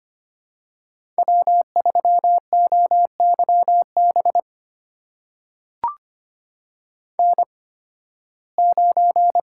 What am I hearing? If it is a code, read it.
Morse code, heard as a keyed tone, about 25 wpm, spaced as for 16 wpm, 700 Hz. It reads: W3OYB N9
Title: USA Call Signs - 5 Characters - 8x Character Spacing - 25wpm